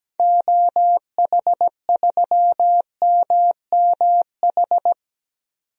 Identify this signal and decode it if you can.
Morse code, heard as a keyed tone, about 17 wpm, 700 Hz. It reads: OH3MMH
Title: DXCC-100 Call Signs - 6 Characters - 17wpm